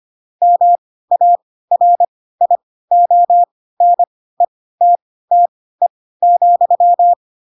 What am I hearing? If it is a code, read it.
Morse code, heard as a keyed tone, about 25 wpm, spaced as for 16 wpm, 700 Hz. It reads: MARIONETTE,
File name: Pinocchio / Pinocchio ch4 10 25 ik1hge